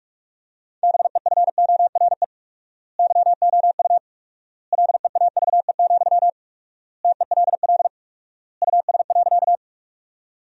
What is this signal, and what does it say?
Morse code, heard as a keyed tone, about 46 wpm, 700 Hz. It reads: BEFORE YOU LEAVE, TELL US.
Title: 4x Word-Spacing - Sentences from Top 400 Words - 45wpm